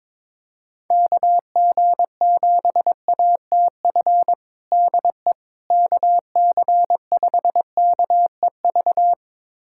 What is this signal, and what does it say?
Morse code, heard as a keyed tone, about 22 wpm, 700 Hz. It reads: KG7ATF DE KC5KEV
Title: USA Callsign DE Callsign - 6 Characters - 22wpm